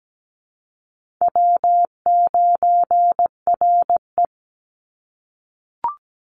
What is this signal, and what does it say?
Morse code, heard as a keyed tone, about 17 wpm, 700 Hz. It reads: W9RE
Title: Easy Call Signs - 17wpm